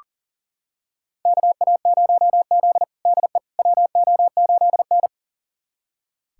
Morse code, heard as a keyed tone, about 37 wpm, 700 Hz. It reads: KA0Z DE WO8N
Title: USA Callsign DE Callsign - 4 Characters - 40wpm